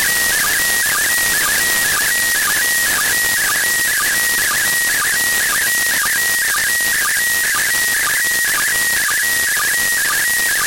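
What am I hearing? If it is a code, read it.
SSTV (slow-scan television), PD 120.
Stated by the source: SSTV signal from the ISS received on 145.803 MHz.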